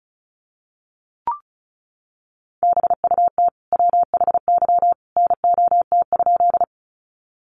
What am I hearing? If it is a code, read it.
Morse code, heard as a keyed tone, about 34 wpm, 700 Hz. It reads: BUT WHY NOT?
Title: Sentences from Top 600 Words 35wpm